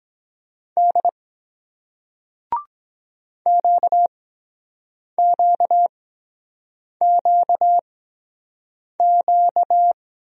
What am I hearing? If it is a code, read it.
Morse code, heard as a keyed tone, about 23 wpm, spaced as for 7 wpm, 700 Hz. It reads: D QQQQ